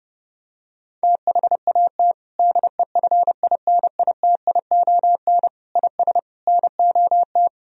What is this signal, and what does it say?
Morse code, heard as a keyed tone, about 30 wpm, 700 Hz. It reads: THAT DEFINITION IS NOT
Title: Sentences from Top 1000 Words - Review Part 4 - 30wpm